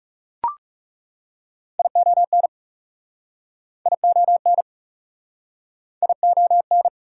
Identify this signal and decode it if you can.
Morse code, heard as a keyed tone, about 37 wpm, 700 Hz. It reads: ION ION ION
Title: Speed-Racing Prefixes and Suffixes 30wpm